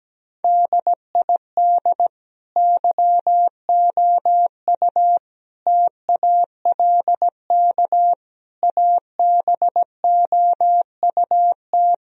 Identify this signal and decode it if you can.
Morse code, heard as a keyed tone, about 17 wpm, 700 Hz. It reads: DID YOU TALK ABOUT
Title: Sentences from Top 1000 Words - Review Part 4 - 17wpm